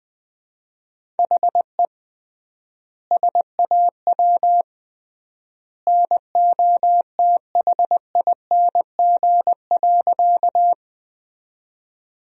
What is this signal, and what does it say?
Morse code, heard as a keyed tone, about 20 wpm, 700 Hz. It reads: HE SAW NOTHING.
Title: 3x Word-Spacing - Sentences from Top 500 Words - 20wpm